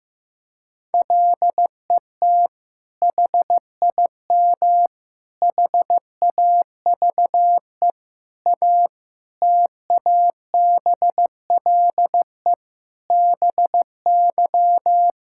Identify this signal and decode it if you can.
Morse code, heard as a keyed tone, about 15 wpm, 700 Hz. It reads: LET HIM HAVE A TABLE BY
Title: Sentences from Top 1600 Words 15wpm